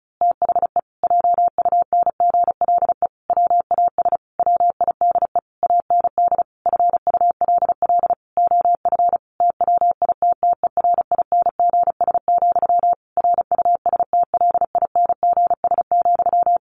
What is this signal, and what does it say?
Morse code, heard as a keyed tone, about 34 wpm, 700 Hz. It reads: THE JUNGLE WAS WIDE AND FULL OF TWITTERINGS, RUSTLINGS,
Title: A Sound Of Thunder 35wpm